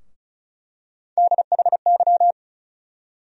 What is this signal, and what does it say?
Morse code, heard as a keyed tone, about 34 wpm, 700 Hz. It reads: DHY
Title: USA Call Sign Postfixes - 3 Letters 35wpm